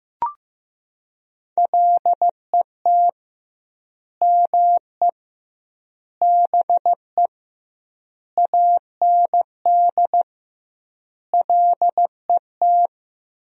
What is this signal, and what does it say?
Morse code, heard as a keyed tone, about 15 wpm, 700 Hz. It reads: LET ME BE AND LET
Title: 2x Word-Spacing - Sentences from Top 300 Words - 15wpm